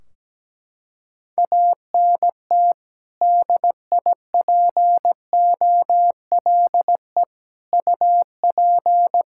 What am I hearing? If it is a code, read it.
Morse code, heard as a keyed tone, about 17 wpm, 700 Hz. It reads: ANT DIPOLE UP
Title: Practice QSO - Exchange 2 - Info - 17wpm